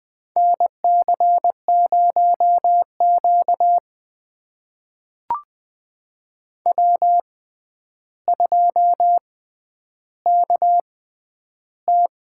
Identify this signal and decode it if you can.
Morse code, heard as a keyed tone, about 21 wpm, spaced as for 10 wpm, 700 Hz. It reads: NC0Q W2KT
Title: USA Call Signs - 4 Characters - 6x Character Spacing - 20wpm